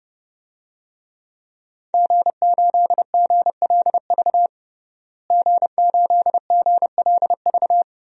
Morse code, heard as a keyed tone, about 30 wpm, 700 Hz. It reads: G8GLV G8GLV
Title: DXCC-100 Call Signs - 5 Characters - Repeated 3x - 30wpm